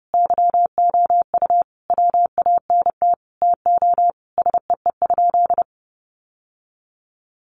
Morse code, heard as a keyed tone, about 30 wpm, 700 Hz. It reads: YOU WANT TO SEE?
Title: Sentences from Top 1100 Words 30wpm